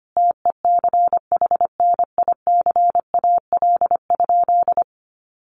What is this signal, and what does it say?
Morse code, heard as a keyed tone, about 25 wpm, 700 Hz. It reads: TECHNICAL?
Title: Sentences from Top 1100 Words 25wpm